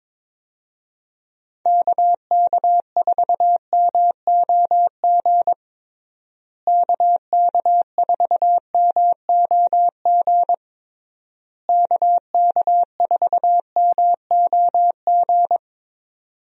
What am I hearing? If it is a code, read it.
Morse code, heard as a keyed tone, about 22 wpm, 700 Hz. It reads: KK4MOG KK4MOG KK4MOG
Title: USA Call Signs - 6 Characters - Repeated 3x - 22wpm